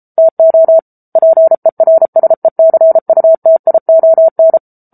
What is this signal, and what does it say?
Morse code, heard as a keyed tone, about 34 wpm, 650 Hz. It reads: TO PERSECUTION
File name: Book Two / 07-Chapter 7 - 33 WPM